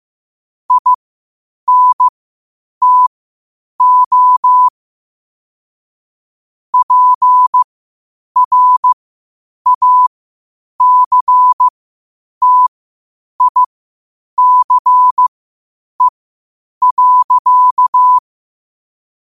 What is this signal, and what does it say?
Morse code, heard as a keyed tone, about 15 wpm, spaced as for 8 wpm, 1000 Hz. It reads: INTO PRACTICE.
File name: qotdfiles / 8wpm